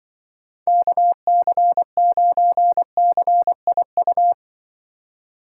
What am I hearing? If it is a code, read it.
Morse code, heard as a keyed tone, about 24 wpm, 700 Hz. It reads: KC9CIU
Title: Speed Racing - USA 2x3 Call Signs 20wpm